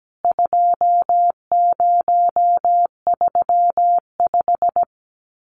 Morse code, heard as a keyed tone, about 17 wpm, 700 Hz. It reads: 2035
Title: CWT Exchange 17wpm